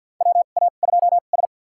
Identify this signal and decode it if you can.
Morse code, heard as a keyed tone, about 51 wpm, spaced as for 36 wpm, 700 Hz. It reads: WA2S
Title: USA Call Signs - 4 Characters - 2x Character Spacing - 50wpm